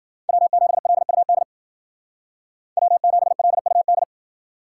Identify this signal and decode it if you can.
Morse code, heard as a keyed tone, about 49 wpm, 700 Hz. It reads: W7LUD W7LUD
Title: Speed Racing - USA 1x3 Call Signs 40wpm